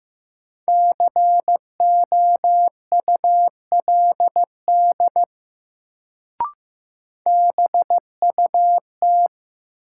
Morse code, heard as a keyed tone, about 15 wpm, 700 Hz. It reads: COULD BUT
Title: Sentences from Top 100 Words - New Word Drill - 15wpm